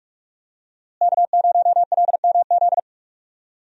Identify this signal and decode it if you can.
Morse code, heard as a keyed tone, about 46 wpm, 700 Hz. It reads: K0LMZ
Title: USA Call Signs - 5 Characters - Repeated 3x - 45wpm